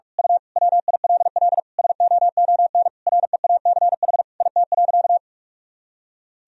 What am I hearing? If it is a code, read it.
Morse code, heard as a keyed tone, about 46 wpm, 700 Hz. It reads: U WILL SOON REACH IT.
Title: Sentences from Top 400 Words and Abbreviations 45wpm